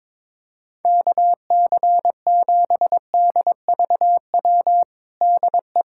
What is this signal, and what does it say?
Morse code, heard as a keyed tone, about 22 wpm, 700 Hz. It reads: KC7DVW DE
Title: USA Callsign DE Callsign 22wpm